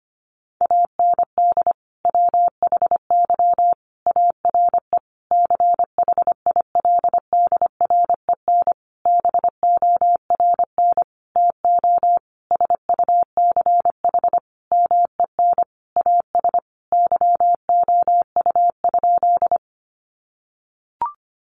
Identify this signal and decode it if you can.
Morse code, heard as a keyed tone, about 25 wpm, 700 Hz. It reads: AND WHY ARE CHILDREN BORN TO SUCH MEN AS YOU?
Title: Sentences from Top 1000 Words 25wpm